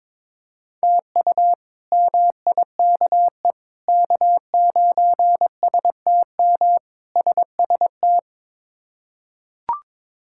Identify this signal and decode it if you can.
Morse code, heard as a keyed tone, about 22 wpm, 700 Hz. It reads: TU MIKE K9STM SST
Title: K1USN SST QSO Practice - 22wpm